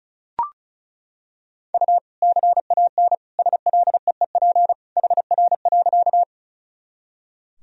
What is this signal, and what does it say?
Morse code, heard as a keyed tone, about 36 wpm, 700 Hz. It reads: U CAN SLEEP HR.
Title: Sentences from Top 500 Words and Abbreviations 35wpm